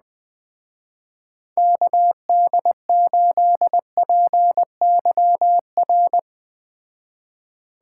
Morse code, heard as a keyed tone, about 20 wpm, 700 Hz. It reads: KD8PYR
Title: Speed Racing - USA 2x3 Call Signs 17wpm